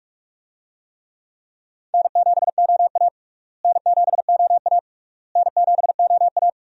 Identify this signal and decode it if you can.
Morse code, heard as a keyed tone, about 46 wpm, 700 Hz. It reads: N7OA N7OA N7OA
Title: USA Call Signs - 4 Characters - Repeated 3x - 45wpm